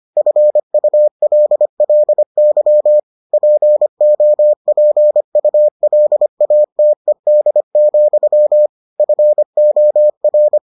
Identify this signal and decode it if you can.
Morse code, heard as a keyed tone, about 25 wpm, 600 Hz. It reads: FULLY POPULATED, FOR